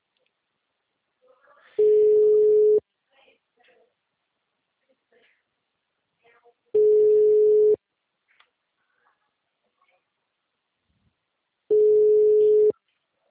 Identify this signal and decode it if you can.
Ringback tone (Europe, 425 Hz), with speech.